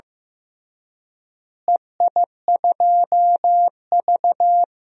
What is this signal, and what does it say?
Morse code, heard as a keyed tone, about 16 wpm, 700 Hz. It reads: EI2V
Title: DXCC-100 Call Signs - 4 Characters - Repeated 3x - 15wpm